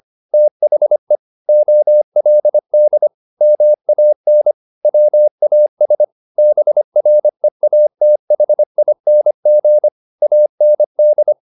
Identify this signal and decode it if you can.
Morse code, heard as a keyed tone, about 25 wpm, 600 Hz. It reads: THE OLD MAN WAS BREATHING AND